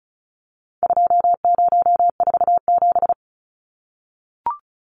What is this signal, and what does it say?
Morse code, heard as a keyed tone, about 34 wpm, 700 Hz. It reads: 2047
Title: Numbers - 4-Digit 35wpm